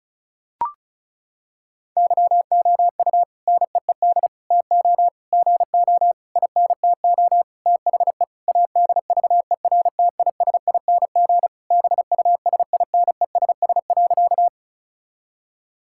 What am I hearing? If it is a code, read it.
Morse code, heard as a keyed tone, about 34 wpm, 700 Hz. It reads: YOU NEED TO GO INTO THE ADVERTISING BUSINESS.
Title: Sentences from Top 1000 Words - Review Part 6 - 35wpm